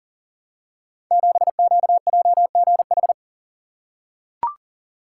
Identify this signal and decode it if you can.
Morse code, heard as a keyed tone, about 39 wpm, 690 Hz. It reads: 7QJGH
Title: Code Groups - 5 Letter:Numbers - 40wpm